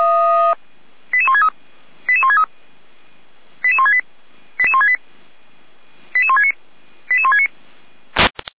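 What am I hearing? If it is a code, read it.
Five-tone selective call (ZVEI), heard as quick runs of five tones, tones of about 80 ms. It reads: 88163 88163 88168 88168 88169 88169
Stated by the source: ZVEI five-tone call received by SDR.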